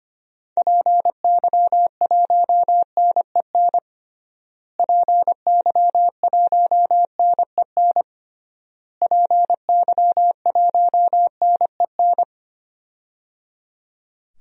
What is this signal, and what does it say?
Morse code, heard as a keyed tone, about 25 wpm, 700 Hz. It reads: PY1NEN PY1NEN PY1NEN
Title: DXCC-100 Call Signs - 6 Characters - Repeated 3x - 25wpm